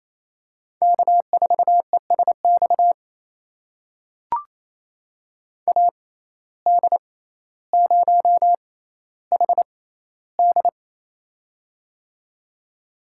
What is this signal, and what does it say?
Morse code, heard as a keyed tone, about 28 wpm, spaced as for 14 wpm, 700 Hz. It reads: K4ESX AD0HD
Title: USA Call Signs - 5 Characters - 6x Character Spacing - 28wpm